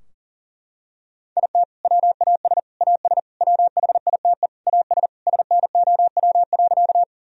Morse code, heard as a keyed tone, about 40 wpm, 700 Hz. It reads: IT WAS AS WHITE AS SNOW.
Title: Sentences from Top 1000 Words - Review Part 3 - 40wpm